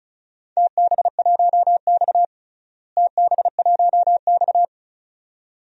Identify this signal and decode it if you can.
Morse code, heard as a keyed tone, about 34 wpm, 700 Hz. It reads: TB1X TB1X
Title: DXCC-100 Call Signs - 4 Characters - Repeated 2x - 35wpm